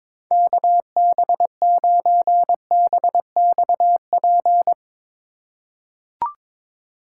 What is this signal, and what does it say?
Morse code, heard as a keyed tone, about 22 wpm, 700 Hz. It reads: KB9BXP
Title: USA Call Signs - 4x Character Spacing - 22wpm